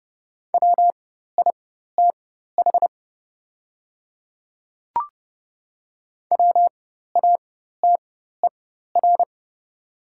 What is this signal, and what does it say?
Morse code, heard as a keyed tone, about 30 wpm, spaced as for 14 wpm, 700 Hz. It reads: WITH WATER
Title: 4x Character Spacing - Top 200 Words - 30wpm